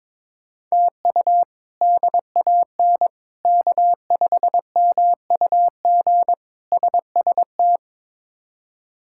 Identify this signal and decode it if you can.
Morse code, heard as a keyed tone, about 22 wpm, 700 Hz. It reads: TU DAN K5MUG SST
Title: K1USN SST QSO Practice - 22wpm